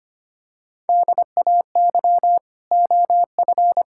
Morse code, heard as a keyed tone, about 25 wpm, 700 Hz. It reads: DAY OF
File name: Sets of 9 Words - 25wpm